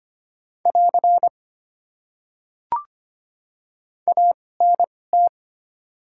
Morse code, heard as a keyed tone, about 25 wpm, spaced as for 18 wpm, 700 Hz. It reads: <AR> ANT
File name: Top 90 QSO Elements - 2x Character Spacing - 25wpm